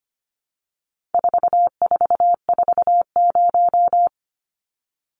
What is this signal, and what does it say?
Morse code, heard as a keyed tone, about 25 wpm, 700 Hz. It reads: 4440
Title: Numbers - 4-Digit 25wpm